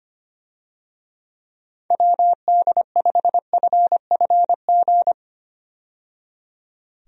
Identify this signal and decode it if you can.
Morse code, heard as a keyed tone, about 25 wpm, 700 Hz. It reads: WD5FFG
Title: USA Call Signs - 6 Characters - 25wpm